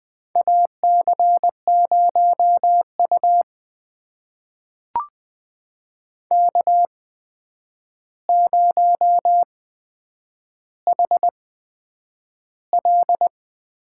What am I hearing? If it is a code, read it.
Morse code, heard as a keyed tone, about 20 wpm, spaced as for 9 wpm, 700 Hz. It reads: AC0U K0HL
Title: USA Call Signs - 4 Characters - 8x Character Spacing - 20wpm